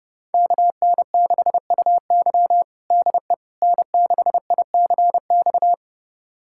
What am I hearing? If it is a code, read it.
Morse code, heard as a keyed tone, about 32 wpm, 700 Hz. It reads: KN6UY DE N6ICX
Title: USA Callsign DE Callsign - 5 Characters - 30wpm